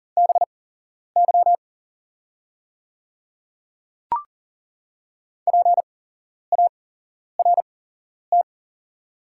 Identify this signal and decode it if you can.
Morse code, heard as a keyed tone, about 40 wpm, spaced as for 11 wpm, 690 Hz. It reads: BY PART
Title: 8x Character Spacing - Top 200 Words - 40wpm